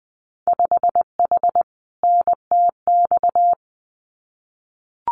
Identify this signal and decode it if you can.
Morse code, heard as a keyed tone, about 20 wpm, 700 Hz. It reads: 5H NTX
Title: Winter Field Day Exchanges Encore 20wpm